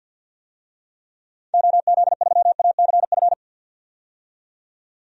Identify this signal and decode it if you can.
Morse code, heard as a keyed tone, about 52 wpm, 700 Hz. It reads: OZ3ACF
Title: DXCC-100 Call Signs - 6 Characters - 50wpm